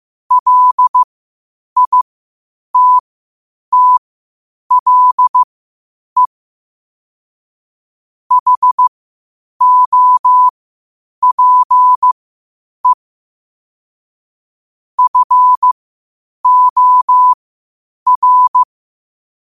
Morse code, heard as a keyed tone, about 15 wpm, spaced as for 8 wpm, 1000 Hz. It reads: LITTLE HOPE FOR
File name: qotdfiles / 8wpm